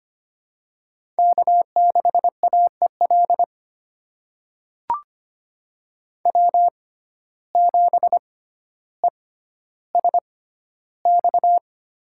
Morse code, heard as a keyed tone, about 25 wpm, spaced as for 13 wpm, 700 Hz. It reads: K6AEL W7ESX
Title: USA Call Signs - 5 Characters - 6x Character Spacing - 25wpm